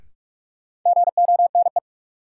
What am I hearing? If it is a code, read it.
Morse code, heard as a keyed tone, about 46 wpm, 700 Hz. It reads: GONE
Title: Sentences from Top 1200 Words - New Word Drill - 45wpm